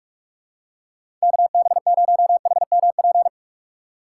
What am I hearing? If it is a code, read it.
Morse code, heard as a keyed tone, about 47 wpm, 700 Hz. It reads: KB0HMP
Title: USA Call Signs - 6 Characters - 4x Character Spacing - 45wpm